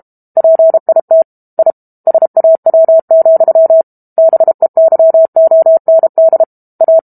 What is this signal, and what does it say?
Morse code, heard as a keyed tone, about 32 wpm, 650 Hz. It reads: PIT I SAW, BEYOND A
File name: Book Two / 06-Chapter 6 - 32 WPM